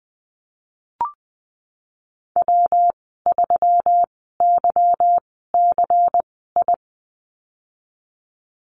Morse code, heard as a keyed tone, about 20 wpm, spaced as for 15 wpm, 700 Hz. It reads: W3YCI
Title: 173 Lesson 28g - Number 9 - Call Signs 20wpm